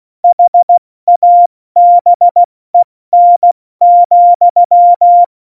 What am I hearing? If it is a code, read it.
Morse code, heard as a keyed tone, about 16 wpm, 700 Hz. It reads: HABEN,
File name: Morsekurs / Witze / Witz 24. November 2020, 70 F80